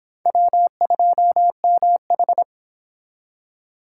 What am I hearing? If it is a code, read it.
Morse code, heard as a keyed tone, about 26 wpm, 700 Hz. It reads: W2MH